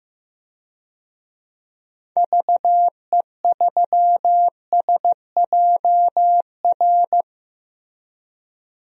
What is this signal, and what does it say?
Morse code, heard as a keyed tone, about 15 wpm, 700 Hz. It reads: VE3SJR
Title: DXCC-100 Call Signs - 6 Characters - 15wpm